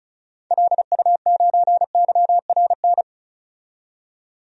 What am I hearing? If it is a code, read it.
Morse code, heard as a keyed tone, about 34 wpm, 700 Hz. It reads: LU9YRN
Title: DXCC-100 Call Signs - 6 Characters - 35wpm